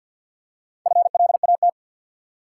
Morse code, heard as a keyed tone, about 53 wpm, 700 Hz. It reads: FLAT
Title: Four Letter Words - 50wpm